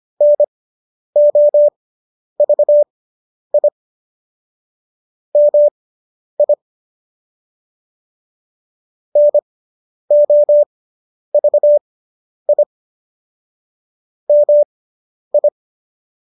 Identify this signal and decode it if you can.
Morse code, heard as a keyed tone, about 25 wpm, spaced as for 10 wpm, 590 Hz. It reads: NOVI MI NOVI MI